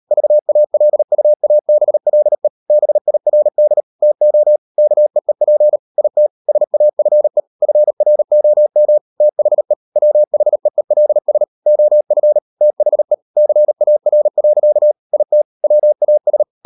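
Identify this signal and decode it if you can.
Morse code, heard as a keyed tone, about 38 wpm, 600 Hz. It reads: VALUABLE BIRD TO KEEP IT SAFE FROM THE WHEELS OF THE CAR. IT WAS
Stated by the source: Morse code practice at 38 WPM, sending Ken Follett's novel Never in English, from CW med Gnister.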